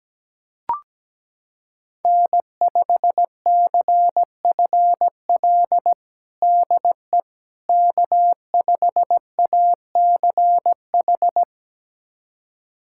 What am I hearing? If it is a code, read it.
Morse code, heard as a keyed tone, about 17 wpm, 700 Hz. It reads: N5CFL DE K5ACH
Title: USA Callsign DE Callsign - 5 Characters - 17wpm